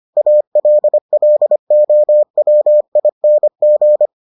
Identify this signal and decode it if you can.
Morse code, heard as a keyed tone, about 25 wpm, 600 Hz. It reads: ALLOWING